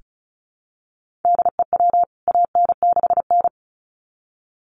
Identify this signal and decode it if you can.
Morse code, heard as a keyed tone, about 34 wpm, 700 Hz. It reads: DEW AN6N
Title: CWT Exchange 35wpm